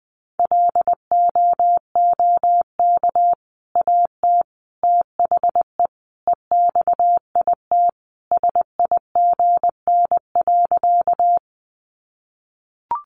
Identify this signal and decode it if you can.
Morse code, heard as a keyed tone, about 20 wpm, 700 Hz. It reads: LOOK AT THE EXIT SIGN.
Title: Sentences from Top 1100 Words 20wpm